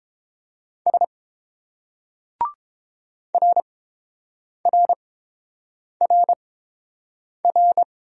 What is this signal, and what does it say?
Morse code, heard as a keyed tone, about 30 wpm, spaced as for 7 wpm, 690 Hz. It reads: S RRRR